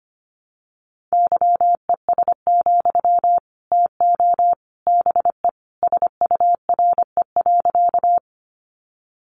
Morse code, heard as a keyed tone, about 25 wpm, 700 Hz. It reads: YES, TO BE SURE.
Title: Sentences from Top 500 Words 25wpm